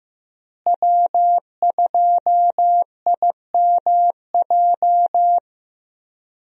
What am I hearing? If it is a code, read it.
Morse code, heard as a keyed tone, about 16 wpm, 700 Hz. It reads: W2IMJ
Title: USA Call Signs - 5 Characters - Repeated 3x - 15wpm